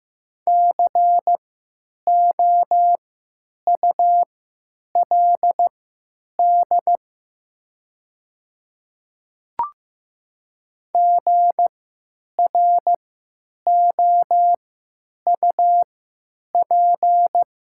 Morse code, heard as a keyed tone, about 15 wpm, spaced as for 9 wpm, 700 Hz. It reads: COULD GROUP
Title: 3x Character Spacing - Top 300 Words - 15wpm